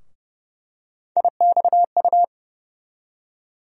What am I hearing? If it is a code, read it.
Morse code, heard as a keyed tone, about 30 wpm, 700 Hz. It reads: IXU
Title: USA Call Sign Postfixes - 3 Letters 30wpm